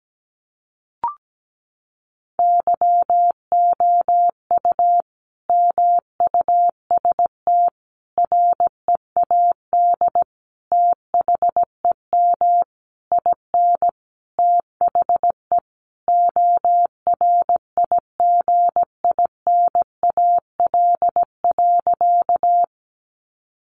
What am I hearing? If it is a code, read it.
Morse code, heard as a keyed tone, about 17 wpm, 700 Hz. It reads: YOU MUST READ THEM IN THE ORIGINAL.
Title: Sentences from Top 900 Words 17wpm